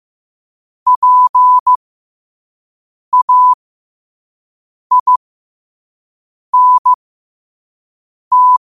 Morse code, heard as a keyed tone, about 15 wpm, spaced as for 5 wpm, 1000 Hz. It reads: PAINT
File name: qotdfiles / 5wpm